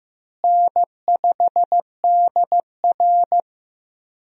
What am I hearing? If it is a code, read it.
Morse code, heard as a keyed tone, about 15 wpm, 700 Hz. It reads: N5DR
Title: USA Call Signs - 4 Characters - Repeated 2x - 15wpm